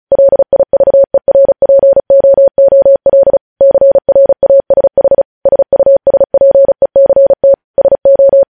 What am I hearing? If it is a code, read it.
Morse code, heard as a keyed tone, about 36 wpm, 550 Hz. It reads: LIVERPOOL CRASH SUSPECT SO
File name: podcast / News headlines-1x-35x35